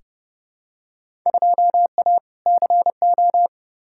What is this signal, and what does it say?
Morse code, heard as a keyed tone, about 30 wpm, 700 Hz. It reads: 2A CO
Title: Field Day Exchanges - 30wpm